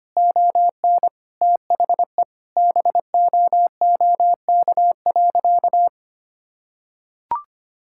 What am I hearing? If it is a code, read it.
Morse code, heard as a keyed tone, about 25 wpm, 700 Hz. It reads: ON THE BOOK.
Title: Sentences from Top 1600 Words 25wpm